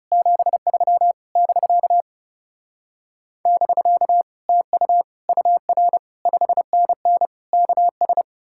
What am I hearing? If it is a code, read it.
Morse code, heard as a keyed tone, about 40 wpm, 700 Hz. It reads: 73 <BK> <BK> TU UR 5NN KS